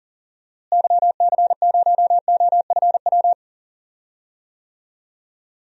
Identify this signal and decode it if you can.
Morse code, heard as a keyed tone, about 40 wpm, 700 Hz. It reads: YC0OFW
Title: DXCC-100 Call Signs - 6 Characters - Repeated 3x - 40wpm